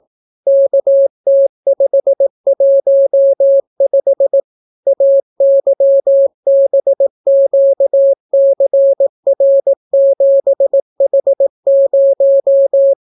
Morse code, heard as a keyed tone, about 18 wpm, 550 Hz. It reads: KT515 AYBQCR7H0